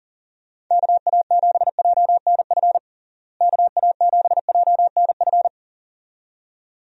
Morse code, heard as a keyed tone, about 40 wpm, 700 Hz. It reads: KA7JNF KA7JNF
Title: USA Call Signs - 6 Characters - Repeated 2x - 40wpm